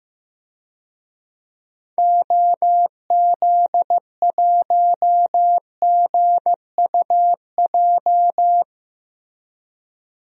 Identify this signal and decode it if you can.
Morse code, heard as a keyed tone, about 16 wpm, 700 Hz. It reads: OZ1GUJ
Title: DXCC-100 Call Signs - 6 Characters - Repeated 3x - 15wpm